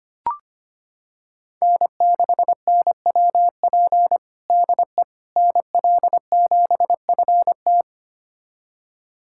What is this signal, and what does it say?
Morse code, heard as a keyed tone, about 25 wpm, 700 Hz. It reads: N6NWP DE NL7FT
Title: USA Callsign DE Callsign - 5 Characters - 25wpm